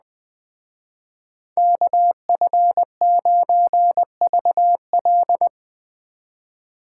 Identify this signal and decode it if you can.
Morse code, heard as a keyed tone, about 20 wpm, 700 Hz. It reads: KF9VL
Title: Speed Racing - USA 2x2 Call Signs 17wpm